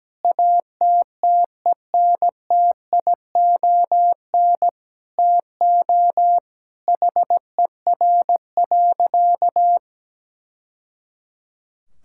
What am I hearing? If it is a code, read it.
Morse code, heard as a keyed tone, about 17 wpm, 700 Hz. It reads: ATTENTION TO HER.
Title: Sentences from Top 1000 Words - Review Part 6 - 17wpm